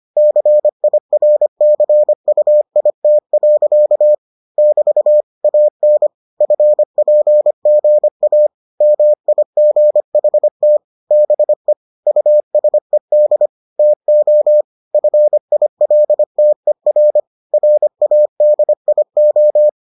Morse code, heard as a keyed tone, about 25 wpm, 600 Hz. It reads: CIRCUIT. <BT> AN FPGA MIGHT BE USED TO FILTER RADIO